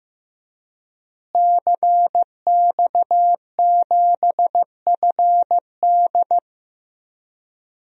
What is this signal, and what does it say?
Morse code, heard as a keyed tone, about 15 wpm, 700 Hz. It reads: CX7FD
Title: DXCC-100 Call Signs - 5 Characters - Repeated 2x - 15wpm